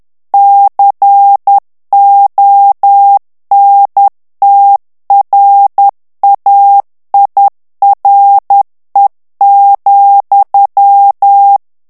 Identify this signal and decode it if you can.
Morse code, heard as a keyed tone, about 10 wpm, 800 Hz. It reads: CONTRAIRE,
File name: morse mp3 / FRANCAIS TEXTE CLAIR 2 VITESSE 10 MOTS MINUTES